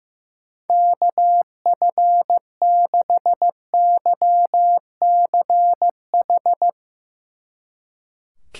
Morse code, heard as a keyed tone, about 16 wpm, 700 Hz. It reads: KF6YCH
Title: Speed Racing - USA Call Signs 15wpm